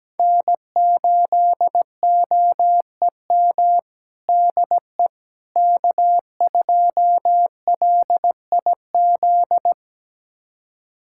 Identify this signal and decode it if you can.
Morse code, heard as a keyed tone, about 17 wpm, 700 Hz. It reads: N8OEM DE K2LIZ
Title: USA Callsign DE Callsign 17wpm